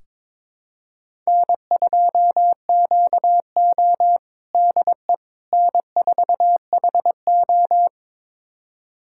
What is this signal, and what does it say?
Morse code, heard as a keyed tone, about 22 wpm, 700 Hz. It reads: N2QO DE N4HO
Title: USA Callsign DE Callsign - 4 Characters - 22wpm